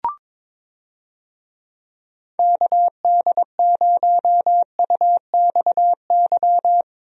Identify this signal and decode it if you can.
Morse code, heard as a keyed tone, about 22 wpm, 700 Hz. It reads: KD0UXY
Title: USA Call Signs - 6 Characters - Repeated 3x - 22wpm